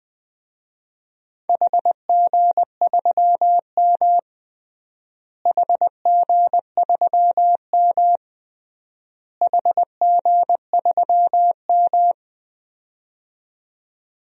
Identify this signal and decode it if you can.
Morse code, heard as a keyed tone, about 20 wpm, 700 Hz. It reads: HG3M HG3M HG3M
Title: DXCC-100 Call Signs - 4 Characters - Repeated 3x - 20wpm